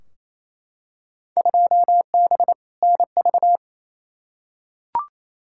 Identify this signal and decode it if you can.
Morse code, heard as a keyed tone, about 28 wpm, 700 Hz. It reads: 2B NV
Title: Field Day Exchanges Encore - 28wpm